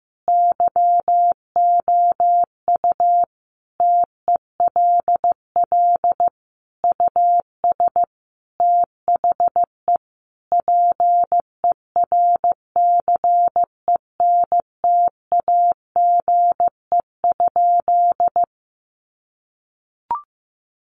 Morse code, heard as a keyed tone, about 15 wpm, 700 Hz. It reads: YOU TELL US THE PERCENTAGE?
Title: Sentences from Top 1300 Words 15wpm